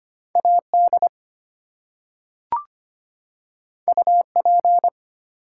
Morse code, heard as a keyed tone, about 25 wpm, 700 Hz. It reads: AD UP
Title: Two Letter Words - 25wpm